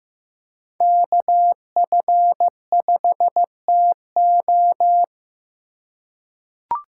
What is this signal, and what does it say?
Morse code, heard as a keyed tone, about 15 wpm, 700 Hz. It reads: KF5TO
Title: USA Call Signs - 5 Characters - 4x Character Spacing - 15wpm